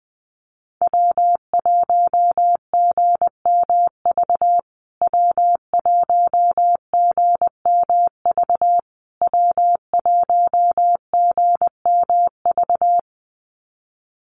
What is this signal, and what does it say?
Morse code, heard as a keyed tone, about 19 wpm, 700 Hz. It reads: W1GMV W1GMV W1GMV
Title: USA Call Signs Repeated 3x 20wpm